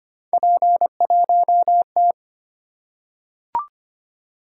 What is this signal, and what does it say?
Morse code, heard as a keyed tone, about 25 wpm, 700 Hz. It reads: P1T
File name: Three Letter-Number Combinations - 25wpm